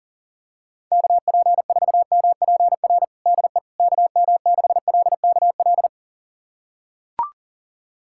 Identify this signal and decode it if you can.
Morse code, heard as a keyed tone, about 40 wpm, 700 Hz. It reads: KP4MPR DE KM6LKL
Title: USA Callsign DE Callsign - 6 Characters - 40wpm